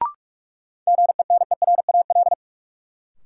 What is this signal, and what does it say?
Morse code, heard as a keyed tone, about 44 wpm, 700 Hz. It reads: GENERAL
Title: Sentences from Top 600 Words - New Word Drill - 45wpm